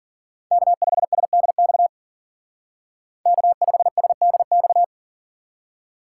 Morse code, heard as a keyed tone, about 48 wpm, 700 Hz. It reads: K5SDX K5SDX
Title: Speed Racing - USA 1x3 Call Signs 40wpm